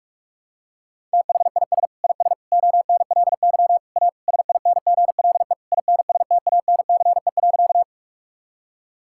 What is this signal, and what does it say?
Morse code, heard as a keyed tone, about 46 wpm, 700 Hz. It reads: THIS IS ONLY A SINGLE INSTANCE.
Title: Sentences from Top 1000 Words - Review Part 5 - 45wpm